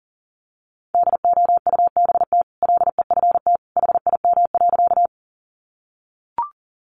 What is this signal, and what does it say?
Morse code, heard as a keyed tone, about 40 wpm, 700 Hz. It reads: DOUBT LEFT HIM.
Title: Sentences from Top 1500 Words 40wpm